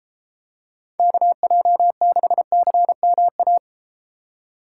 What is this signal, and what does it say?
Morse code, heard as a keyed tone, about 32 wpm, 700 Hz. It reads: KJ6CMA